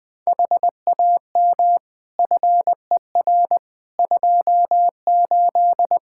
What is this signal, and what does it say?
Morse code, heard as a keyed tone, about 20 wpm, 700 Hz. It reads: HAM FER 28
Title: K1USN SST QSO Practice 20wpm